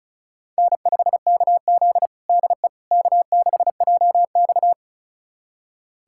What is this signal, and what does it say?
Morse code, heard as a keyed tone, about 36 wpm, 700 Hz. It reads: N5KZ DE K6JX
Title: USA Callsign DE Callsign - 4 Characters - 35wpm